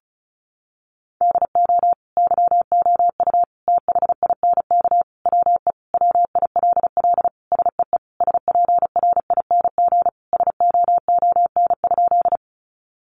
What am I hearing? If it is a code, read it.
Morse code, heard as a keyed tone, about 34 wpm, 700 Hz. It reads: DO YOU THINK WE WILL SEE SPRING SOON?
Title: Sentences from Top 1200 Words 35wpm